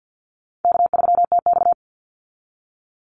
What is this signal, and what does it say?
Morse code, heard as a keyed tone, about 50 wpm, 700 Hz. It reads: K3TX
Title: Easy Call Signs Repeated 2x - 50wpm